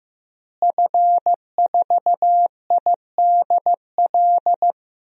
Morse code, heard as a keyed tone, about 15 wpm, 700 Hz. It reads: F4IDL
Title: DXCC-100 Call Signs - 5 Characters - Repeated 2x - 15wpm